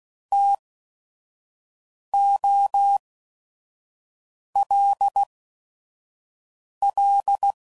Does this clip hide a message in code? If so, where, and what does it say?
Yes: Morse code from 0.3 to 7.5 s, heard as a keyed tone, about 16 wpm, spaced as for 5 wpm, 800 Hz. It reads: TOLL